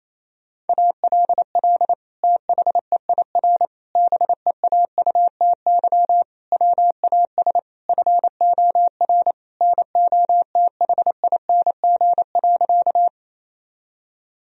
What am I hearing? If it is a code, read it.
Morse code, heard as a keyed tone, about 28 wpm, 700 Hz. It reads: ALL THEIR BEAUTY WAS FOR NOTHING.
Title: Sentences from Top 1000 Words - Review Part 2 - 28wpm